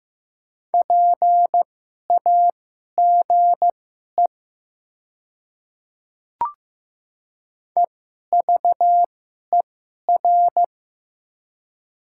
Morse code, heard as a keyed tone, about 16 wpm, spaced as for 11 wpm, 700 Hz. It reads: PAGE EVER
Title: 2x Character Spacing - Top 500 Words - 15wpm